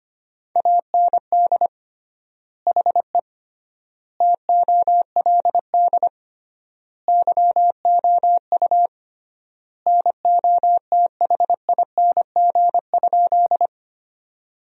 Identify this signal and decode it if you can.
Morse code, heard as a keyed tone, about 25 wpm, 700 Hz. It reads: AND HE TOLD YOU NOTHING?
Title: 3x Word-Spacing - Sentences from Top 500 Words - 25wpm